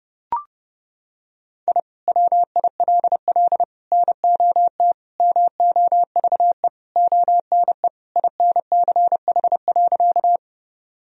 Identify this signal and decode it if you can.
Morse code, heard as a keyed tone, about 32 wpm, 700 Hz. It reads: I WILL NOT MOVE ONE INCH.
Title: Sentences from Top 1000 Words - Review Part 3 - 30wpm